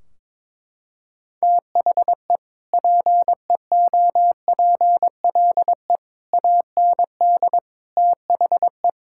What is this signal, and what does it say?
Morse code, heard as a keyed tone, about 22 wpm, 700 Hz. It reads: THE PEOPLE AND THE
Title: Sentences from Top 1600 Words 22wpm